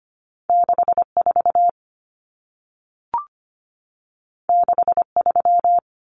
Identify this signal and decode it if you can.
Morse code, heard as a keyed tone, about 25 wpm, 700 Hz. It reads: 64 63
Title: Numbers - 2-Digit 25wpm